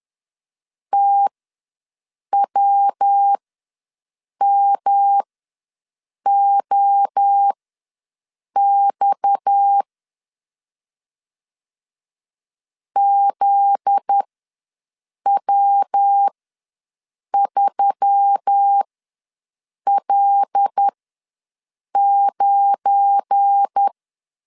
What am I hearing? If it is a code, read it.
Morse code, heard as a keyed tone, about 10 wpm, spaced as for 6 wpm, 800 Hz. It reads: TWMOX ZW3L9
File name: LEZIONI / 8 YJ45 / 8 Y ws 10wpm 3952